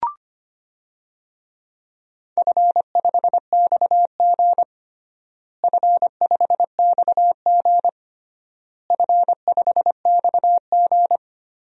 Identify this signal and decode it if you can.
Morse code, heard as a keyed tone, about 25 wpm, 700 Hz. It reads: F5XG F5XG F5XG
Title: DXCC-100 Call Signs - 4 Characters - Repeated 3x - 25wpm